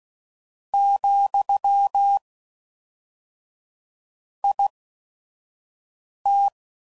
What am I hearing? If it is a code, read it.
Morse code, heard as a keyed tone, about 16 wpm, spaced as for 5 wpm, 800 Hz. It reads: , IT
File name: CW Code Groups @5WPM / Technical Subjects / AND Gate